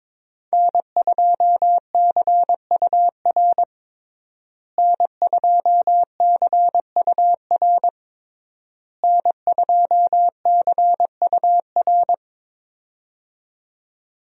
Morse code, heard as a keyed tone, about 21 wpm, 700 Hz. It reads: N2CUR N2CUR N2CUR
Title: USA Call Signs - 5 Characters - Repeated 3x - 22wpm